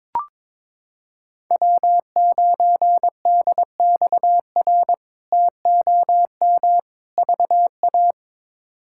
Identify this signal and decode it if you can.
Morse code, heard as a keyed tone, about 21 wpm, 700 Hz. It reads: W9DXR TOM VA
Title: K1USN SST QSO Practice - 22wpm